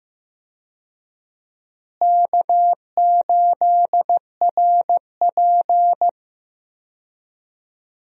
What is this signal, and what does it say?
Morse code, heard as a keyed tone, about 15 wpm, 700 Hz. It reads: K8RP
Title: USA Call Signs - 4 Characters - Repeated 3x - 15wpm